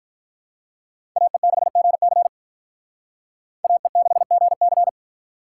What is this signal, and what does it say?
Morse code, heard as a keyed tone, about 45 wpm, 700 Hz. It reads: AE6GC AE6GC